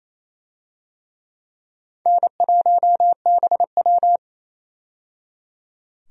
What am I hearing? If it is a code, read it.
Morse code, heard as a keyed tone, about 28 wpm, 700 Hz. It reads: N1BW
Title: USA Call Signs - 4 Characters - 28wpm